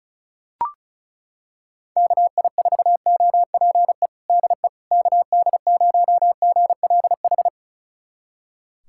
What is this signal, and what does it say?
Morse code, heard as a keyed tone, about 36 wpm, 700 Hz. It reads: KI4OPE DE KD0GLH
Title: USA Callsign DE Callsign 35wpm